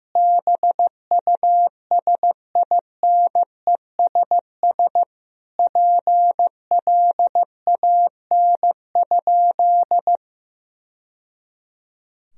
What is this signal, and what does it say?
Morse code, heard as a keyed tone, about 16 wpm, 700 Hz. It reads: BUSINESS PLAN?
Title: Sentences from Top 1000 Words - Review Part 3 - 15wpm